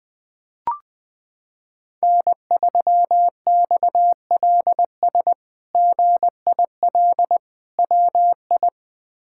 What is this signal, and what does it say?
Morse code, heard as a keyed tone, about 20 wpm, 700 Hz. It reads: N3XLS GIL WI
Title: K1USN SST QSO Practice - 20wpm